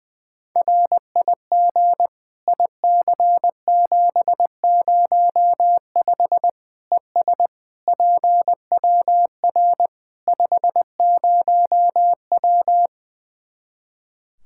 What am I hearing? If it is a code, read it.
Morse code, heard as a keyed tone, about 20 wpm, 700 Hz. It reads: RIG IC705 ES PWR 50W
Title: K1USN SST QSO Practice 20wpm